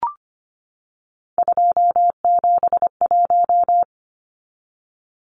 Morse code, heard as a keyed tone, about 25 wpm, 690 Hz. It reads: 271
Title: Numbers - 3-Digit 25wpm